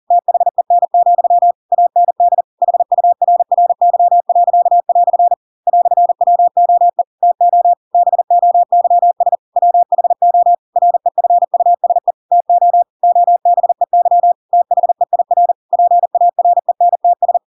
Morse code, heard as a keyed tone, about 40 wpm, 700 Hz. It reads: THEN, AND HURRY." "WOE TO BOYS WHO REFUSE TO OBEY THEIR PARENTS
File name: Pinocchio / Pinocchio ch4 40 40 ik1hge